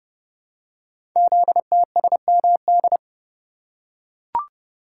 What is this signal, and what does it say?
Morse code, heard as a keyed tone, about 30 wpm, 700 Hz. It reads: ZTSMD
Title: Code Groups - 5 Letters - 30wpm